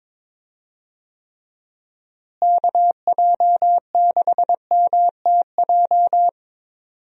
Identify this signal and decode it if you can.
Morse code, heard as a keyed tone, about 22 wpm, 700 Hz. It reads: KJ6MTJ
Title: USA Call Signs - 6 Characters - Repeated 3x - 22wpm